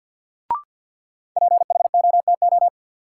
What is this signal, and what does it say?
Morse code, heard as a keyed tone, about 51 wpm, 700 Hz. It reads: PHOTO
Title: Sentences from Top 1200 Words - New Word Drill Alt - 50wpm